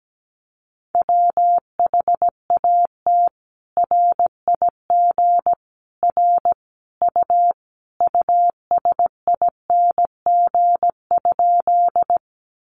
Morse code, heard as a keyed tone, about 17 wpm, 700 Hz. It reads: WHAT RIG R U USING?
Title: Common QSO Phrases 17wpm